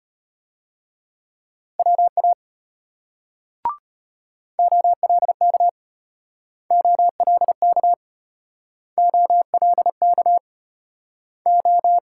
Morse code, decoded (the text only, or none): WA OLK OLK OLK O